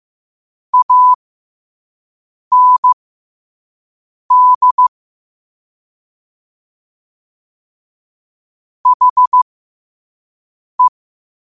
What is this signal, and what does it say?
Morse code, heard as a keyed tone, about 15 wpm, spaced as for 5 wpm, 1000 Hz. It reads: AND HE
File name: qotdfiles / 5wpm